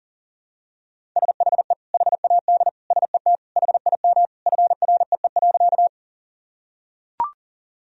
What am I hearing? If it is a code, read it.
Morse code, heard as a keyed tone, about 40 wpm, 700 Hz. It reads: SHE HAD SET HIM FREE.
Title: Sentences from Top 1000 Words - Review Part 1 - 40wpm